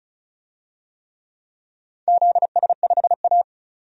Morse code, heard as a keyed tone, about 36 wpm, 700 Hz. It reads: ZS5A
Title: DXCC-100 Call Signs - 4 Characters - 35wpm